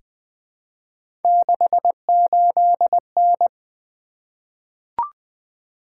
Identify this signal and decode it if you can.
Morse code, heard as a keyed tone, about 20 wpm, 700 Hz. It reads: 68N
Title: Three Letter-Number Combinations - 20wpm